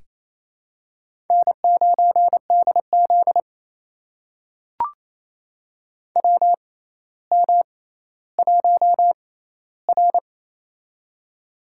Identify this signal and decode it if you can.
Morse code, heard as a keyed tone, about 27 wpm, spaced as for 14 wpm, 700 Hz. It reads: N9DZ WM1R
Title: USA Call Signs - 4 Characters - 6x Character Spacing - 28wpm